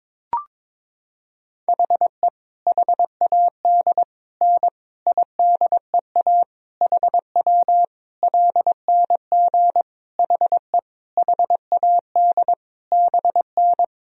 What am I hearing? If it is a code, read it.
Morse code, heard as a keyed tone, about 22 wpm, 700 Hz. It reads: HE HAD N IDEA HW LNG HE HAD BN
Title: Sentences from Top 300 Words and Abbreviations 22wpm